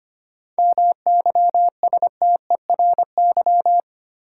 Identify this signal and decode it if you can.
Morse code, heard as a keyed tone, about 25 wpm, 700 Hz. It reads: MYSTERY
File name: practice / 25wpm / Difficult Words with Hints - 25wpm